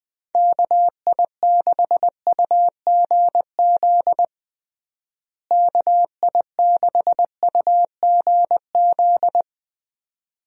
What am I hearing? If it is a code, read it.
Morse code, heard as a keyed tone, about 21 wpm, 700 Hz. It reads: KI6UGZ KI6UGZ
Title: USA Call Signs - 6 Characters - Repeated 2x - 20wpm